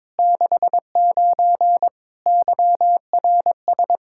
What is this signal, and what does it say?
Morse code, heard as a keyed tone, about 22 wpm, 700 Hz. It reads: 69 YRS
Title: Practice QSO - Exchange 3 - More Info - 22wpm